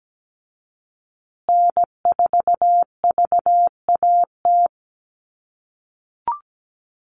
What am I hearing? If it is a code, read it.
Morse code, heard as a keyed tone, about 17 wpm, 700 Hz. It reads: N4VAT
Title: USA Call Signs Repeated 3x 17wpm